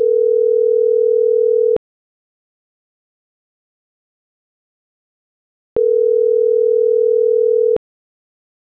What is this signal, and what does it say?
Ringback tone (North America), 440 + 480 Hz.